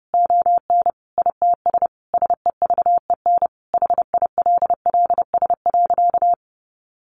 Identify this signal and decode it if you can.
Morse code, heard as a keyed tone, about 30 wpm, 700 Hz. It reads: ON ITS SEVEN HILLS.
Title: A Sound Of Thunder 30wpm